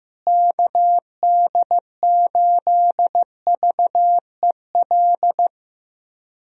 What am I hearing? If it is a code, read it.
Morse code, heard as a keyed tone, about 15 wpm, 700 Hz. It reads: KD8VEL
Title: USA Call Signs - 6 Characters - Repeated 3x - 15wpm